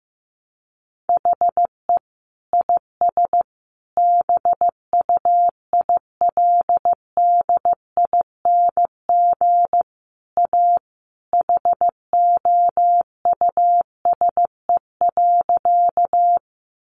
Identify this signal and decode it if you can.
Morse code, heard as a keyed tone, about 14 wpm, 700 Hz. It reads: HE IS BUILDING A HOUSE.
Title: Sentences from Top 700 Words 15wpm